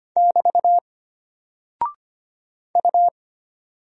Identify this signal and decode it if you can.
Morse code, heard as a keyed tone, about 25 wpm, 700 Hz. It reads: <BT> U
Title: Single Character (Letter:Number:Punctuation:Prosign) Alt 25wpm